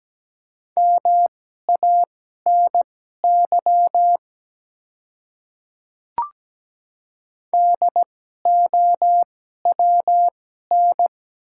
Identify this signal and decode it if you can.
Morse code, heard as a keyed tone, about 17 wpm, spaced as for 12 wpm, 700 Hz. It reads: MANY DOWN
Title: 2x Character Spacing - Top 100 Words - 17wpm